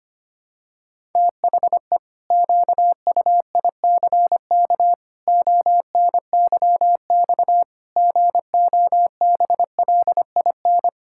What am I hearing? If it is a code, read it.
Morse code, heard as a keyed tone, about 25 wpm, 700 Hz. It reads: THE QUICK ONYX GOBLIN
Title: Pangram Challenge v2 - 25wpm